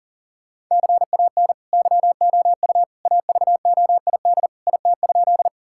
Morse code, heard as a keyed tone, about 40 wpm, 700 Hz. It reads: CAN YOU AVOID IT?
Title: Sentences from Top 1000 Words - Review Part 6 - 40wpm